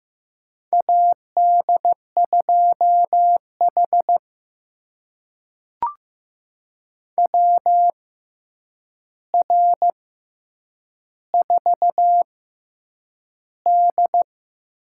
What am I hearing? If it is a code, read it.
Morse code, heard as a keyed tone, about 15 wpm, spaced as for 8 wpm, 700 Hz. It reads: AD2H WR4D
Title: USA Call Signs - 4 Characters - 6x Character Spacing - 15wpm